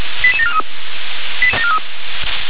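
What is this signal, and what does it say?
Five-tone selective call (ZVEI), heard as quick runs of five tones, tones of about 70 ms. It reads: 88743
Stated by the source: ZVEI five-tone call 88824 received by SDR.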